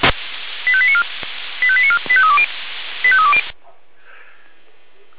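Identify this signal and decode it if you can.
Five-tone selective call (ZVEI), heard as quick runs of five tones, tones of about 70 ms. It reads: 85894 85894 85320 85320
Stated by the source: Five-tone ZVEI selective call, code 85312, received by SDR.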